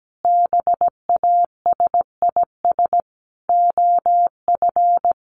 Morse code, heard as a keyed tone, about 17 wpm, 700 Hz. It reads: BASIS OF
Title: Sentences from Top 1200 Words 17wpm